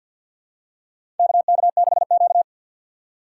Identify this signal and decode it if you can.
Morse code, heard as a keyed tone, about 47 wpm, 690 Hz. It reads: KK6Q
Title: USA Call Signs - 4 Characters - 6x Character Spacing - 50wpm